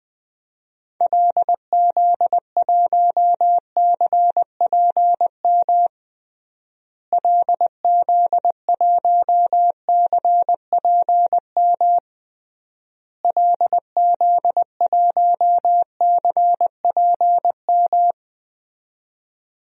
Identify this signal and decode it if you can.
Morse code, heard as a keyed tone, about 20 wpm, 700 Hz. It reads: LZ1CPM LZ1CPM LZ1CPM
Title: DXCC-100 Call Signs - 6 Characters - Repeated 3x - 20wpm